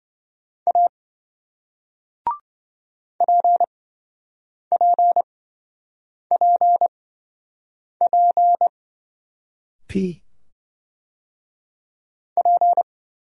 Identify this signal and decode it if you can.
Morse code, heard as a keyed tone, about 26 wpm, spaced as for 7 wpm, 700 Hz. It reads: A PPPP P